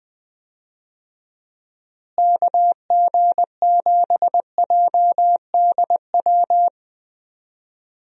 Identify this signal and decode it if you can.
Morse code, heard as a keyed tone, about 20 wpm, 700 Hz. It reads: KG7JDW
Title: USA Call Signs - 6 Characters - Repeated 2x - 20wpm